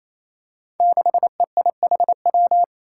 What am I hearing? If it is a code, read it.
Morse code, heard as a keyed tone, about 28 wpm, 700 Hz. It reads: 6EIHW
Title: Five Challenging Letter-Number Combinations 28wpm